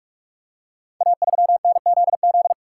Morse code, heard as a keyed tone, about 44 wpm, 700 Hz. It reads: A3NZZ
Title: Code Groups - 5 Letter:Numbers - 45wpm